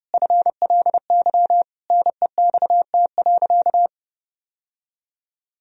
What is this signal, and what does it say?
Morse code, heard as a keyed tone, about 30 wpm, 700 Hz. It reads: FLY NEXT.
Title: Sentences from Top 1000 Words - Review Part 2 - 30wpm